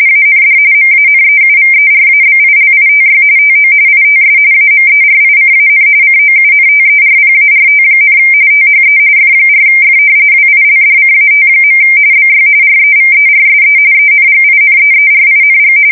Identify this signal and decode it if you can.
RTTY, about 45.45 baud. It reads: AND MANEUVERABILITY TO ITS AMERICAN COUNTERPART, THE F4F WILDCAT. BY REVERSE ENGINEERING
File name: fsk170 / usnavy4